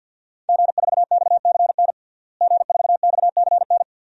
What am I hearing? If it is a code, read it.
Morse code, heard as a keyed tone, about 51 wpm, 700 Hz. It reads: G4XCN G4XCN
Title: DXCC-100 Call Signs - 5 Characters - Repeated 2x - 50wpm